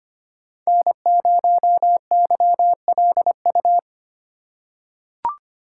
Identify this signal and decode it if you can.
Morse code, heard as a keyed tone, about 25 wpm, 700 Hz. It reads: N0YLU
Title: USA Call Signs - 5 Characters - 2x Character Spacing - 25wpm